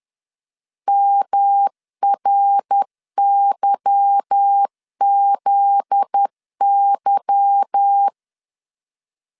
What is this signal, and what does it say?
Morse code, heard as a keyed tone, about 10 wpm, 800 Hz. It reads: MRYZY
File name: LEZIONI / 8 YJ45 / 8 Y 10wpm 2615